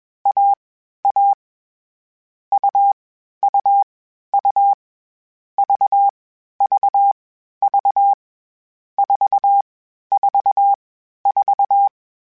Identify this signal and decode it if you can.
Morse code, heard as a keyed tone, about 21 wpm, 800 Hz. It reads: A A U U U V V V 4 4 4
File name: CW Code Groups-mp3s / Sets mp3s / AUV AUV4 sp @20